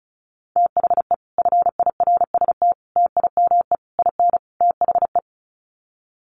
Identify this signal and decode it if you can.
Morse code, heard as a keyed tone, about 34 wpm, 700 Hz. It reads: THE FIRST TIME IN THE
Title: Sets of 5 Words Encore 35wpm